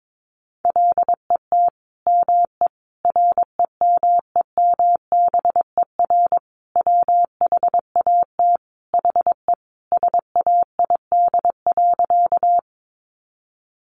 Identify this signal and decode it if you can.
Morse code, heard as a keyed tone, about 22 wpm, 700 Hz. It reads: LET ME REMEMBER WHAT HE SAID.
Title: Sentences from Top 400 Words 22wpm